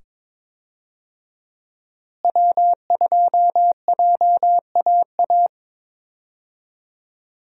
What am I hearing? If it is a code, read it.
Morse code, heard as a keyed tone, about 22 wpm, 700 Hz. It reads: W2JAA
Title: USA Call Signs - 5 Characters - Repeated 2x - 22wpm